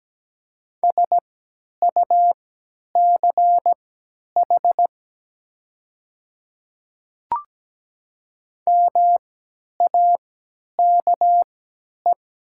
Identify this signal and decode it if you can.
Morse code, heard as a keyed tone, about 17 wpm, spaced as for 10 wpm, 700 Hz. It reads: SUCH MAKE
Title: 3x Character Spacing - Top 500 Words - 17wpm